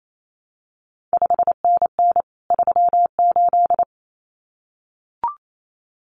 Morse code, heard as a keyed tone, about 28 wpm, 700 Hz. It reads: 5NN 38
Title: CQ WW DX Contest Exchange Encore - 28wpm